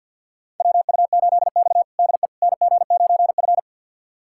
Morse code, heard as a keyed tone, about 50 wpm, 700 Hz. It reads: WU8X DE NG9F
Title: USA Callsign DE Callsign - 4 Characters - 50wpm